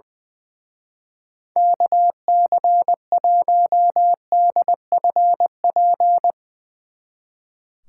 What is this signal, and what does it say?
Morse code, heard as a keyed tone, about 20 wpm, 700 Hz. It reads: KC1DFP
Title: Speed Racing - USA 2x3 Call Signs 20wpm